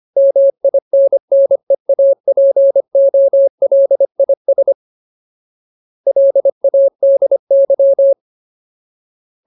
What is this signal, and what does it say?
Morse code, heard as a keyed tone, about 25 wpm, 550 Hz. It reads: MINNEAPOLIS LADY
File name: Ultimate Word List / 25WPM Ultimate